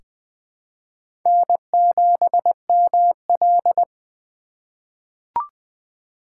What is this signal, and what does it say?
Morse code, heard as a keyed tone, about 20 wpm, 700 Hz. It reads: N7ML
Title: K1USN SST QSO Practice - 20wpm